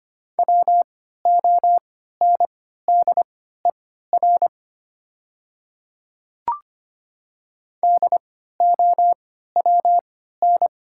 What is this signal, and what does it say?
Morse code, heard as a keyed tone, about 25 wpm, spaced as for 14 wpm, 700 Hz. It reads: WONDER DOWN
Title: 3x Character Spacing - Top 500 Words - 25wpm